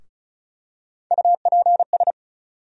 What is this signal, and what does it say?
Morse code, heard as a keyed tone, about 36 wpm, 700 Hz. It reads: UPS
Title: USA Call Sign Postfixes - 3 Letters 35wpm